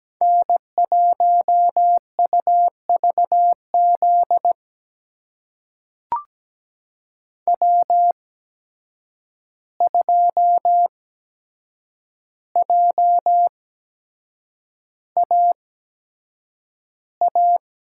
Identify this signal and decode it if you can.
Morse code, heard as a keyed tone, about 17 wpm, spaced as for 7 wpm, 700 Hz. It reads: N1UVZ W2JAA
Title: USA Call Signs - 5 Characters - 8x Character Spacing - 17wpm